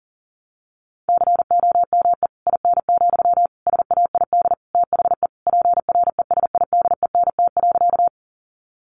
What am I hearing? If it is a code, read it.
Morse code, heard as a keyed tone, about 40 wpm, 700 Hz. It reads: COME IN, SAID THE PRESIDENT.
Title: Sentences from Top 1000 Words 40wpm